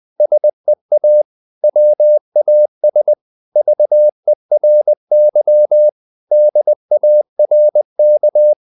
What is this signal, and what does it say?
Morse code, heard as a keyed tone, about 20 wpm, 600 Hz. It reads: SEA WAS VERY DARK
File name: the old man and the sea / the old man and the sea part 1 20wpm